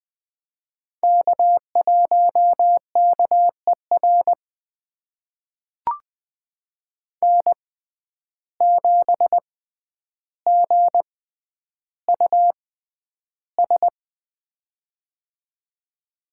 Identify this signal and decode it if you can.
Morse code, heard as a keyed tone, about 20 wpm, spaced as for 10 wpm, 700 Hz. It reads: K1KER N7GUS
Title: USA Call Signs - 5 Characters - 6x Character Spacing - 20wpm